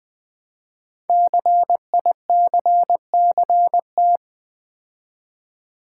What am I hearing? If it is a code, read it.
Morse code, heard as a keyed tone, about 20 wpm, 700 Hz. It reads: CICCT
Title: Code Groups - 5 Letters - 20wpm